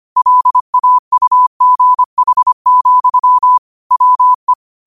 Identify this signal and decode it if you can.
Morse code, heard as a keyed tone, about 25 wpm, 1000 Hz. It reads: LAUGH, WE
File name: qotdfiles / 25wpm